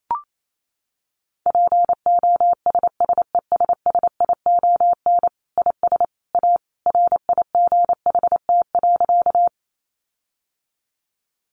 Morse code, heard as a keyed tone, about 28 wpm, 700 Hz. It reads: POSSESSION IS A RIGHT.
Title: Sentences from Top 1400 Words 28wpm